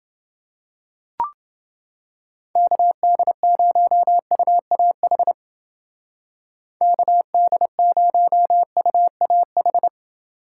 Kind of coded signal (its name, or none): Morse code